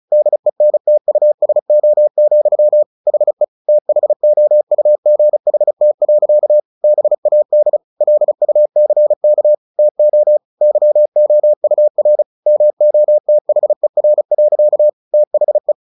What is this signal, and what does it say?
Morse code, heard as a keyed tone, about 34 wpm, 600 Hz. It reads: DENTUSO, HE THOUGHT. BAD LUCK TO YOUR MOTHER. THE
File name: the old man and the sea 2 35wpm